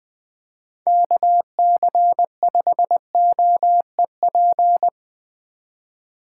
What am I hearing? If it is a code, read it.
Morse code, heard as a keyed tone, about 19 wpm, 700 Hz. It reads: KC5OEP